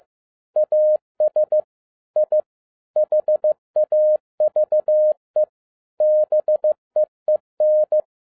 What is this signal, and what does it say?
Morse code, heard as a keyed tone, about 15 wpm, 610 Hz. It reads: AS I HAVE BEEN
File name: 2011-07-mp3 / n47 JOHN K3TN 15wpm